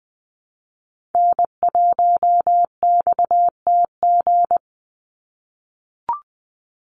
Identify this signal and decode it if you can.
Morse code, heard as a keyed tone, about 20 wpm, 700 Hz. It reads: N1XTG
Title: USA Call Signs 20wpm